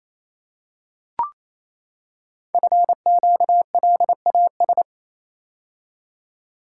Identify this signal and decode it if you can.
Morse code, heard as a keyed tone, about 29 wpm, 700 Hz. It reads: FQLAS
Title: Code Groups - 5 Letters - 28wpm